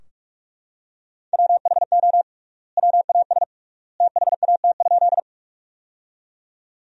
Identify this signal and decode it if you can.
Morse code, heard as a keyed tone, about 46 wpm, 700 Hz. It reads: WHO WAS THAT?
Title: 3x Word-Spacing - Sentences from Top 100 Words - 45wpm